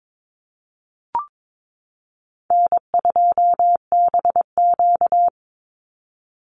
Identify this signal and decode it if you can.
Morse code, heard as a keyed tone, about 22 wpm, 700 Hz. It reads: N2BQ
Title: USA Call Signs 22wpm